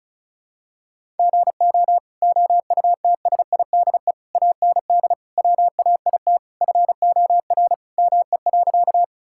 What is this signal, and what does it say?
Morse code, heard as a keyed tone, about 36 wpm, 700 Hz. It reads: GO OUTSIDE AND WAIT FOR ME.
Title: Sentences from Top 1000 Words - Review Part 6 - 35wpm